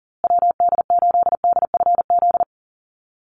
Morse code, heard as a keyed tone, about 40 wpm, 700 Hz. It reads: WD8DFZ
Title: USA Call Signs Repeated 2x 40wpm